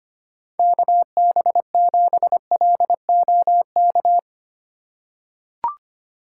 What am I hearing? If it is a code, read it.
Morse code, heard as a keyed tone, about 25 wpm, 700 Hz. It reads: KB7LOK
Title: USA Call Signs - 6 Characters - 2x Character Spacing - 25wpm